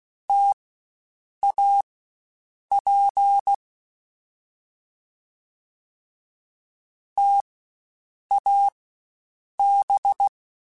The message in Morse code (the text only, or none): TAP TAB